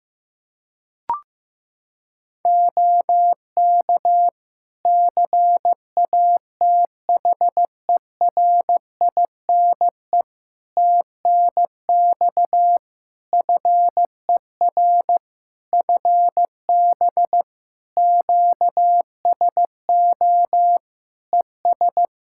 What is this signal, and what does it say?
Morse code, heard as a keyed tone, about 15 wpm, 700 Hz. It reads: OK CATHERINE TNX FER FB QSO ES
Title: Practice QSO - Exchange 4 - Ending - 15wpm